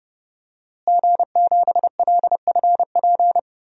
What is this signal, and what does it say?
Morse code, heard as a keyed tone, about 30 wpm, 700 Hz. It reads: G7LFP
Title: DXCC-100 Call Signs - 5 Characters - Repeated 3x - 30wpm